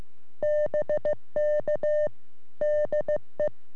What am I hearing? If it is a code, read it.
Morse code, heard as a keyed tone, about 15 wpm, 600 Hz. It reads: BK DE
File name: sm7lcb / g3xdy 3cm